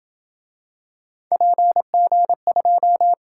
Morse code, heard as a keyed tone, about 26 wpm, 700 Hz. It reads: PG2